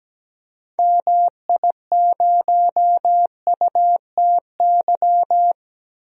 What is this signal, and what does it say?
Morse code, heard as a keyed tone, about 17 wpm, 700 Hz. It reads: MI0UTY
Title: DXCC-100 Call Signs - 6 Characters - 17wpm